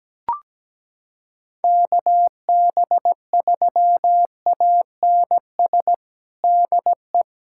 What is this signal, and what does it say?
Morse code, heard as a keyed tone, about 17 wpm, 700 Hz. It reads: KB3ANS DE
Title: USA Callsign DE Callsign - 6 Characters - 17wpm